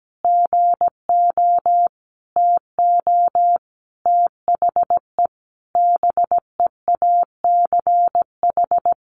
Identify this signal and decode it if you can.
Morse code, heard as a keyed tone, about 17 wpm, 700 Hz. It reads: GO TO THE BEACH
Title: Sentences from Top 1500 Words 17wpm